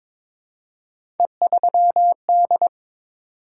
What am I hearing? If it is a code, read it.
Morse code, heard as a keyed tone, about 22 wpm, 700 Hz. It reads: E3D
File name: practice / 22wpm / Three Letter-Number Combinations - 22wpm